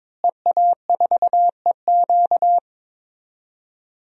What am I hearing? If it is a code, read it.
Morse code, heard as a keyed tone, about 22 wpm, 700 Hz. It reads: EA4EQ
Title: DXCC-100 Call Signs - 5 Characters - Repeated 3x - 22wpm